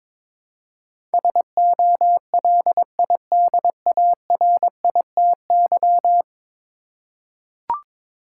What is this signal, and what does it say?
Morse code, heard as a keyed tone, about 21 wpm, 700 Hz. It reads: SOLIDARITY
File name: practice / 22wpm / 10 Letter Words - First 3 Letters Given - 22wpm